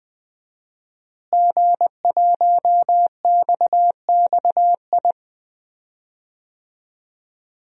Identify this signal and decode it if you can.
Morse code, heard as a keyed tone, about 20 wpm, 700 Hz. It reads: G1XXI
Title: DXCC-100 Call Signs - 5 Characters - Repeated 2x - 20wpm